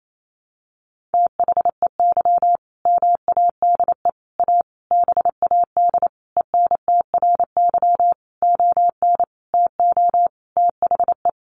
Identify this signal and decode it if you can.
Morse code, heard as a keyed tone, about 28 wpm, 700 Hz. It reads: THEY MADE A BAD ENTRY ON TO THE
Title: Sentences from Top 1500 Words 28wpm